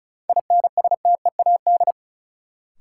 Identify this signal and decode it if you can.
Morse code, heard as a keyed tone, about 34 wpm, 700 Hz. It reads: INSTEAD
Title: Sentences from Top 800 Words - New Word Drill - 35wpm